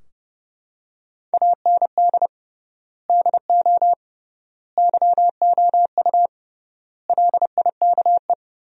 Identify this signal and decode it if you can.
Morse code, heard as a keyed tone, about 30 wpm, 700 Hz. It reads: AND DO YOU LIKE
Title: 3x Word-Spacing - Sentences from Top 100 Words - 30wpm